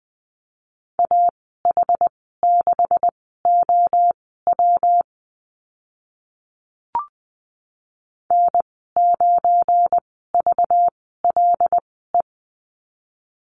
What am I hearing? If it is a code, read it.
Morse code, heard as a keyed tone, about 20 wpm, spaced as for 14 wpm, 700 Hz. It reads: AH6OW N9VLE
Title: 193 Lesson 31g - Letter v - Call Signs 20wpm